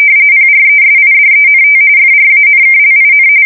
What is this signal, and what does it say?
RTTY, about 45.45 baud. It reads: PRESIDENT FRANKLIN
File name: fsk170 / usnavy4